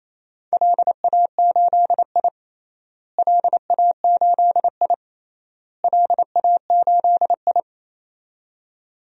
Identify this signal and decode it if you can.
Morse code, heard as a keyed tone, about 28 wpm, 700 Hz. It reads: LA8I LA8I LA8I
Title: DXCC-100 Call Signs - 4 Characters - Repeated 3x - 28wpm